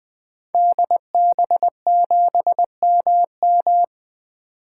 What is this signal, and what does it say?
Morse code, heard as a keyed tone, about 21 wpm, 700 Hz. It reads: DB7MM
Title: 2023 SOTA Activator Call Signs - 20wpm